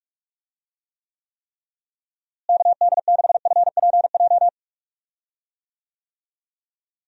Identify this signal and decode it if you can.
Morse code, heard as a keyed tone, about 45 wpm, 700 Hz. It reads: KD6FPJ
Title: USA Call Signs - 6 Characters - Repeated 3x - 45wpm